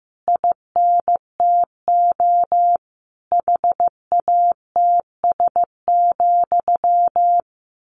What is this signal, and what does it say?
Morse code, heard as a keyed tone, about 15 wpm, 700 Hz. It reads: INTO HATS,
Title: A Sound Of Thunder 15wpm